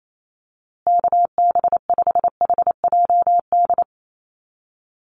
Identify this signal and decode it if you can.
Morse code, heard as a keyed tone, about 28 wpm, 700 Hz. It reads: KB5HJD
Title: USA Call Signs Repeated 2x 28wpm